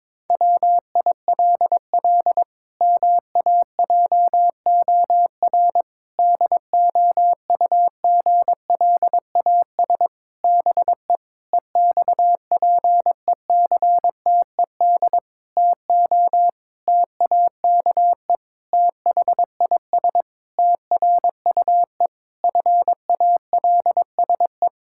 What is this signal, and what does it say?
Morse code, heard as a keyed tone, about 22 wpm, 700 Hz. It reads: WILL MAJOR DOUGLAS BE EXPECTED TO TAKE THIS TRUE FALSE
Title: Pangram Challenge v2 - 22wpm